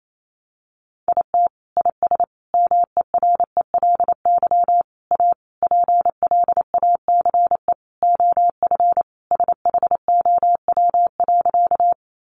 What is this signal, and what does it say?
Morse code, heard as a keyed tone, about 28 wpm, 700 Hz. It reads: IT IS MERELY A PLACE OF SHOW.
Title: Sentences from Top 1400 Words 28wpm